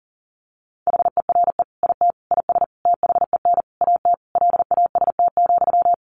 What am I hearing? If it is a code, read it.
Morse code, heard as a keyed tone, about 37 wpm, 700 Hz. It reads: HERE IT IS THEN AT LAST,
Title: Sentences from Top 1500 Words 40wpm